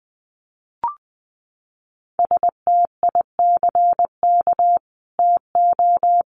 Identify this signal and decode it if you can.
Morse code, heard as a keyed tone, about 20 wpm, 700 Hz. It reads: STICK TO
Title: Sentences from Top 1200 Words 20wpm